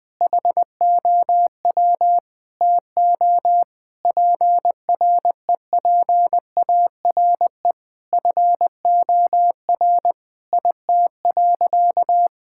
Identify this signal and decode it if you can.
Morse code, heard as a keyed tone, about 21 wpm, 700 Hz. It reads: HOW TO PREPARE FOR IT.
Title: Sentences from Top 1000 Words - Review Part 6 - 20wpm